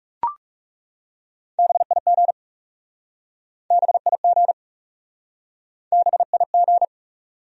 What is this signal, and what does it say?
Morse code, heard as a keyed tone, about 40 wpm, 700 Hz. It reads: BIG BIG BIG